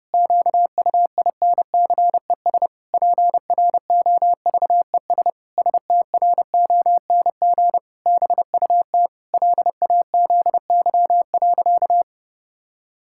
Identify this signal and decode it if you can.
Morse code, heard as a keyed tone, about 32 wpm, 700 Hz. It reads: QUINCES PROVES STRONG BUT LAZY.
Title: Pangram Challenge v2 - 30wpm